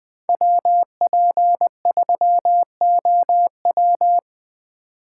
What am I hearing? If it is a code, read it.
Morse code, heard as a keyed tone, about 25 wpm, 700 Hz. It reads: WP3OW
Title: Speed Racing - USA 2x2 Call Signs 20wpm